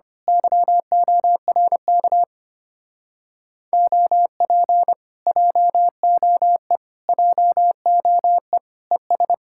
Morse code, heard as a keyed tone, about 24 wpm, 700 Hz. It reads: YORK OP JOE JOE ES
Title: Speed-Racing Common QSO Phrases - 25wpm